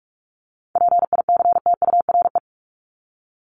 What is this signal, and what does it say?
Morse code, heard as a keyed tone, about 44 wpm, 700 Hz. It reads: PICTURE
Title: Top 500 Words Repeated 45wpm